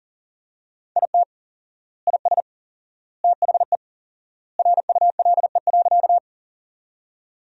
Morse code, heard as a keyed tone, about 40 wpm, 700 Hz. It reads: IT IS THE RULE.
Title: 4x Word-Spacing - Sentences from Top 400 Words - 40wpm